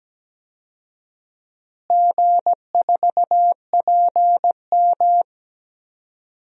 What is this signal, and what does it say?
Morse code, heard as a keyed tone, about 17 wpm, 700 Hz. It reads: G4PM
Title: DXCC-100 Call Signs - 4 Characters - Repeated 3x - 17wpm